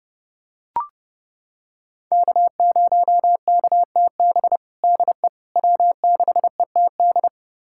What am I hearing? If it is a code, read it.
Morse code, heard as a keyed tone, about 30 wpm, 700 Hz. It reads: K0KTB DE W6ETD
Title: USA Callsign DE Callsign - 5 Characters - 30wpm